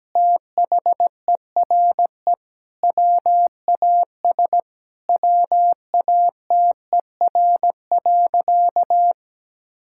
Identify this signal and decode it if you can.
Morse code, heard as a keyed tone, about 17 wpm, 700 Hz. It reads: THERE WAS WATER.
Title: Sentences from Top 1600 Words 17wpm